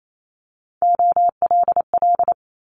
Morse code, heard as a keyed tone, about 28 wpm, 700 Hz. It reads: OLL
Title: Multi-letter Phonemes Quick Fire 28wpm